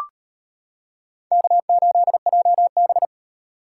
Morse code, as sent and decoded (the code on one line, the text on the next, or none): -.- ---.. .--- -...
K8JB